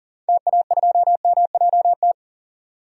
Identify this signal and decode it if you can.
Morse code, heard as a keyed tone, about 40 wpm, 700 Hz. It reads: TA2MJT
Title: DXCC-100 Call Signs - 6 Characters - 40wpm